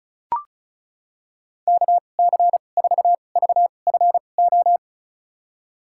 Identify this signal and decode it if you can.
Morse code, heard as a keyed tone, about 34 wpm, spaced as for 25 wpm, 700 Hz. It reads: KC4VFO
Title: USA Call Signs - 2x Character Spacing - 35wpm